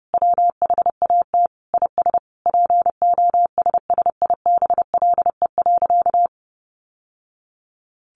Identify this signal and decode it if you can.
Morse code, heard as a keyed tone, about 28 wpm, 700 Hz. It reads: WHAT IS POSSIBLE.
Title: Sentences from Top 900 Words 30wpm